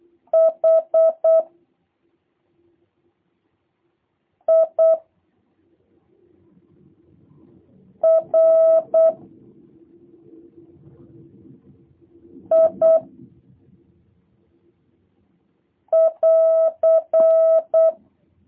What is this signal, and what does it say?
Morse code, heard as a keyed tone, about 8 wpm, 650 Hz. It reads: H I R I <AR>
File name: jh8cbh / 2023cw / review1-2